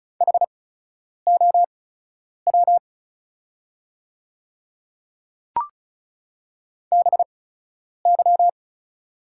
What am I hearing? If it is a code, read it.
Morse code, heard as a keyed tone, about 36 wpm, spaced as for 10 wpm, 700 Hz. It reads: HOW BY
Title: 8x Character Spacing - Top 300 Words - 35wpm